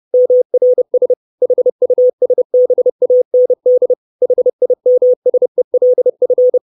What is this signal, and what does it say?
Morse code, heard as a keyed tone, about 30 wpm, 500 Hz. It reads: MRS HUSBAND HIMSELF
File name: 632 Words 30WPM